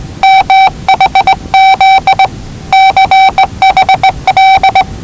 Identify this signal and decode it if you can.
Morse code, heard as a keyed tone, about 20 wpm, 770 Hz. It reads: MHZ CHL
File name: WellingtonRadioZlwCloseDownOnHfAndMf / 01-Zlw 4277khz 417.5 qrt until 0925z